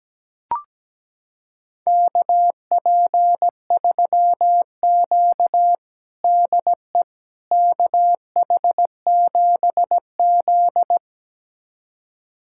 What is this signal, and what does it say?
Morse code, heard as a keyed tone, about 17 wpm, 700 Hz. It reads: KP3Q DE KH7Z
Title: USA Callsign DE Callsign - 4 Characters - 17wpm